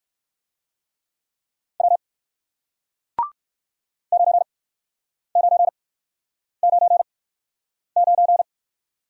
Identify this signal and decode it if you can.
Morse code, heard as a keyed tone, about 60 wpm, spaced as for 9 wpm, 700 Hz. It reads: W 9999